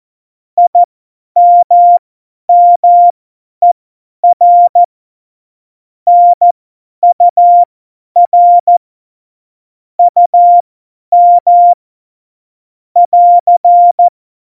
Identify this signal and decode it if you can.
Morse code, heard as a keyed tone, about 14 wpm, spaced as for 10 wpm, 700 Hz. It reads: IMMER NUR UM <AR>
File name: Morsekurs / Witze / Witz 10. Oktober 17, Tempo 50